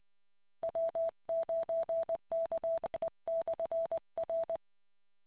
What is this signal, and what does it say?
Morse code, heard as a keyed tone, about 20 wpm, 660 Hz. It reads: W9KE/R